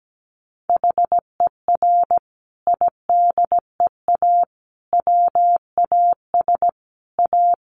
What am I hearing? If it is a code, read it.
Morse code, heard as a keyed tone, about 17 wpm, 700 Hz. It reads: HER IDEA WAS A
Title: Sentences from Top 900 Words 17wpm